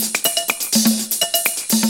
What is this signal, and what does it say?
There is no code.